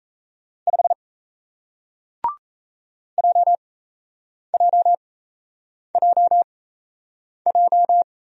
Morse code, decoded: 5 JJJJ

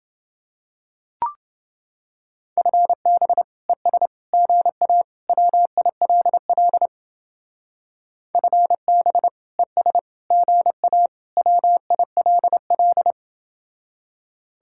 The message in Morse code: FB ES GA WILL FB ES GA WILL

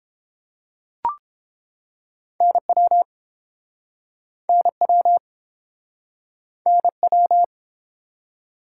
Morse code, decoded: NW NW NW

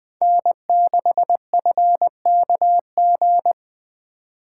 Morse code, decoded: N6FKG

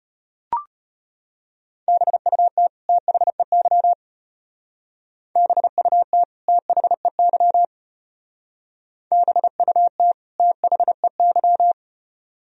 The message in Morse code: BUT THEY BUT THEY BUT THEY

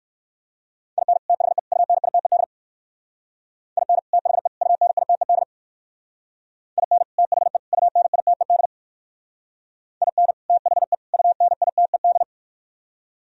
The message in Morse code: IN THE UNITED IN THE UNITED IN THE UNITED IN THE UNITED

